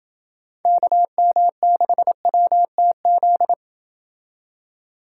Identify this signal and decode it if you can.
Morse code, heard as a keyed tone, about 28 wpm, 700 Hz. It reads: KM6WTZ